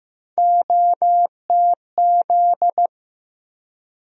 Morse code, heard as a keyed tone, about 15 wpm, 700 Hz. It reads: OTZ
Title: USA Call Sign Postfixes - 3 Letters 15wpm